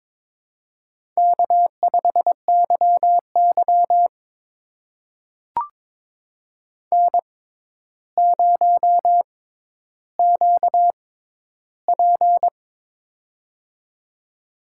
Morse code, heard as a keyed tone, about 23 wpm, spaced as for 11 wpm, 700 Hz. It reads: K5YY N0QP
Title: USA Call Signs - 4 Characters - 6x Character Spacing - 22wpm